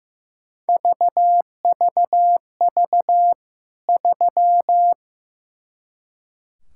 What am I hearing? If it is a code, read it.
Morse code, heard as a keyed tone, about 15 wpm, 700 Hz. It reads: VVV 3
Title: K1USN SST QSO Practice 15wpm